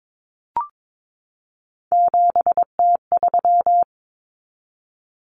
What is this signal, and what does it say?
Morse code, heard as a keyed tone, about 22 wpm, 700 Hz. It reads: 7T3
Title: Common Cut Numbers - 3-Digit 22wpm